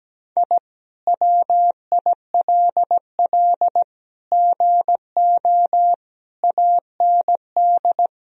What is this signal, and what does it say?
Morse code, heard as a keyed tone, about 17 wpm, 700 Hz. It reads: I WILL GO AND
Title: Sentences from Top 1000 Words - Review Part 1 - 17wpm